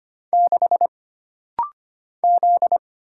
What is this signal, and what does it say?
Morse code, heard as a keyed tone, about 25 wpm, 700 Hz. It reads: 6 Z
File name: Single Character (Letter:Number:Punctuation:Prosign) - Rapid-Fire - 25wpm